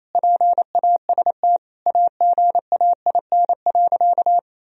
Morse code, heard as a keyed tone, about 27 wpm, 700 Hz. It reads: PAST AGAIN.
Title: Sentences from Top 1000 Words - Review Part 3 - 28wpm